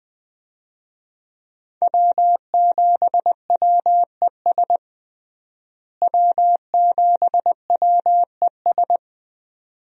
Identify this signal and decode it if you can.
Morse code, heard as a keyed tone, about 21 wpm, 700 Hz. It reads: W7WES W7WES
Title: USA Call Signs - 5 Characters - Repeated 2x - 20wpm